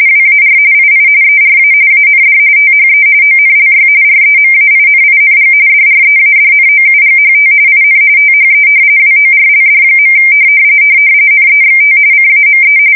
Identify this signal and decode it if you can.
RTTY, about 45.45 baud. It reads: WAR II ERA AND THE ONLY REMAINING DREADNOUGHT IN THE WORLD. NEXT CAME
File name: fsk170 / battleships